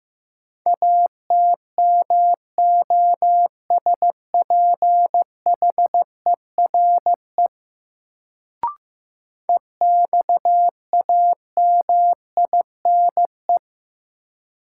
Morse code, heard as a keyed tone, about 15 wpm, 700 Hz. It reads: ATMOSPHERE EXAMINE
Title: Sentences from Top 1600 Words - New Word Drill - 15wpm